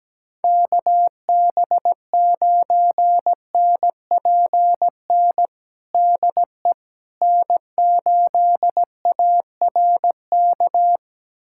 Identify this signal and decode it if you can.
Morse code, heard as a keyed tone, about 17 wpm, 700 Hz. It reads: KB9NPN DE N8ARK
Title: USA Callsign DE Callsign 17wpm